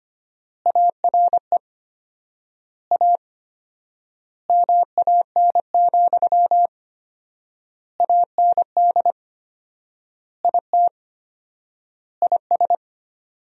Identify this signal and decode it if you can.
Morse code, heard as a keyed tone, about 25 wpm, 700 Hz. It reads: ARE A MAN, AND IT IS
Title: 4x Word-Spacing - Sentences from Top 200 Words - 25wpm